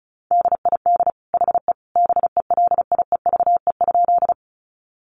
Morse code, heard as a keyed tone, about 34 wpm, 700 Hz. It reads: DID HE BELIEVE?
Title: Sentences from Top 1000 Words 35wpm